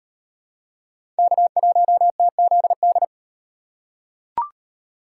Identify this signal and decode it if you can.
Morse code, heard as a keyed tone, about 35 wpm, 700 Hz. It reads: K1TZD